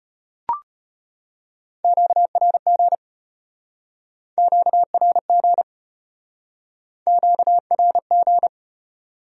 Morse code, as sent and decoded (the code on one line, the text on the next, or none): --.- .-. --. / --.- .-. --. / --.- .-. --.
QRG QRG QRG